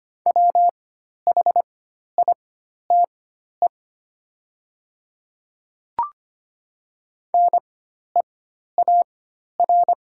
Morse code, heard as a keyed tone, about 25 wpm, spaced as for 12 wpm, 690 Hz. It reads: WHITE NEAR
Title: 4x Character Spacing - Top 300 Words - 25wpm